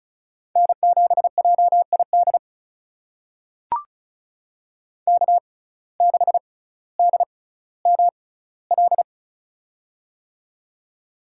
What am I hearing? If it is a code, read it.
Morse code, heard as a keyed tone, about 36 wpm, spaced as for 18 wpm, 700 Hz. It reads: N7JID K6DML
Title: USA Call Signs - 5 Characters - 6x Character Spacing - 35wpm